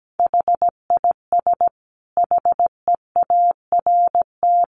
Morse code, heard as a keyed tone, about 17 wpm, 700 Hz. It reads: HIS HEART
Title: Sentences from Top 1500 Words 17wpm